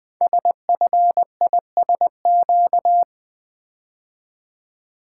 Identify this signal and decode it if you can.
Morse code, heard as a keyed tone, about 21 wpm, 700 Hz. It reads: SFISQ
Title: Code Groups - 5 Letter:Numbers - 20wpm